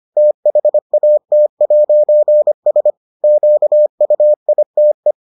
Morse code, heard as a keyed tone, about 25 wpm, 600 Hz. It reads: THAT'S QUITE